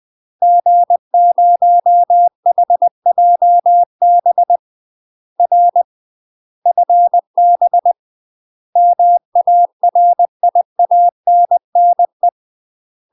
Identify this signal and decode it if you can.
Morse code, heard as a keyed tone, about 20 wpm, 700 Hz. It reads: G0HJB R FB MARIANNE
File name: UK MASTER QSOs TRAINING / 20WPM / UK MASTER QSOs 0020